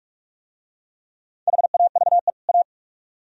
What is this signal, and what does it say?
Morse code, heard as a keyed tone, about 44 wpm, 700 Hz. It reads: HAVE A